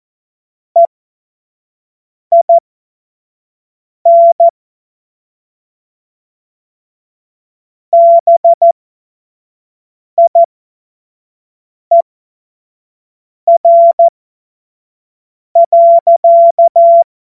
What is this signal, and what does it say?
Morse code, heard as a keyed tone, about 14 wpm, spaced as for 5 wpm, 680 Hz. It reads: EIN BIER.
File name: Morsekurs / Witze / Witz 18. Mai 2021, 25 F70